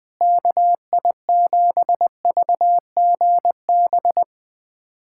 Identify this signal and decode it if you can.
Morse code, heard as a keyed tone, about 20 wpm, 700 Hz. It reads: KI7VGB
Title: USA Callsign DE Callsign - 6 Characters - 20wpm